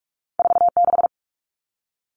Morse code, heard as a keyed tone, about 46 wpm, 700 Hz. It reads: 46
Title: All Cut Numbers - 2-Digit 45wpm